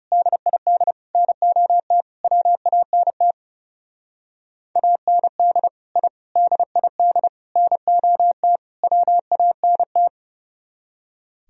Morse code, heard as a keyed tone, about 37 wpm, 700 Hz. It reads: DID NOT WANT AND I DID NOT WANT
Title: Speed-Racing - Sets of 5 Words 30wpm